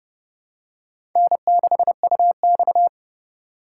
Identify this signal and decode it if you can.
Morse code, heard as a keyed tone, about 30 wpm, 700 Hz. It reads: N6UX
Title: USA Call Signs - 4 Characters - 3x Character Spacing - 30wpm